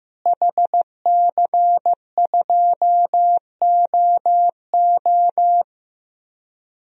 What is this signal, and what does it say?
Morse code, heard as a keyed tone, about 15 wpm, 700 Hz. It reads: HC2OO
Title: DXCC-100 Call Signs - 5 Characters - Repeated 2x - 15wpm